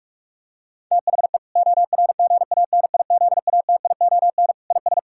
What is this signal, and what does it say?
Morse code, heard as a keyed tone, about 42 wpm, 700 Hz. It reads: THE ORGANIZATION IS
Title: Sentences from Top 1000 Words - Review Part 5 - 45wpm